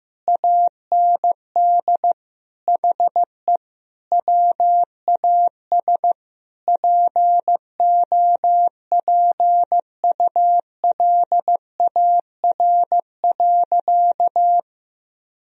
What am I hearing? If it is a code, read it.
Morse code, heard as a keyed tone, about 15 wpm, 700 Hz. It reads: AND HE WAS POPULAR.
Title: Sentences from Top 1000 Words - Review Part 3 - 15wpm